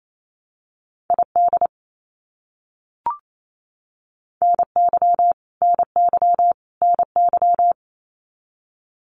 Morse code, heard as a keyed tone, about 28 wpm, 700 Hz. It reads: ID NY NY NY
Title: US States Repeated 3x 28wpm